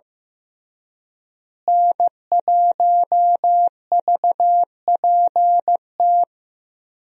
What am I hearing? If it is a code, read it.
Morse code, heard as a keyed tone, about 15 wpm, 700 Hz. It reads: N1VPT
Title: USA Call Signs - 5 Characters - Repeated 3x - 15wpm